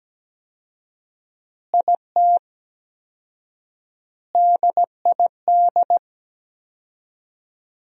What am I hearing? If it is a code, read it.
Morse code, heard as a keyed tone, about 17 wpm, 700 Hz. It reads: IT DID
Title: 4x Word-Spacing - Sentences from Top 200 Words - 17wpm